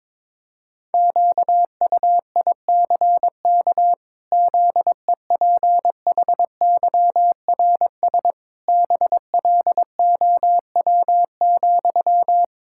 Morse code, heard as a keyed tone, about 21 wpm, 700 Hz. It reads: QUICK ZEPHYRS BLOW,
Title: Pangram Challenge v2 - 22wpm